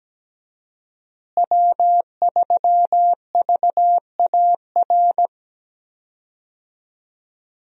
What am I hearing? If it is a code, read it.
Morse code, heard as a keyed tone, about 17 wpm, 700 Hz. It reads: W3VAR
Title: USA Call Signs - 5 Characters - Repeated 3x - 17wpm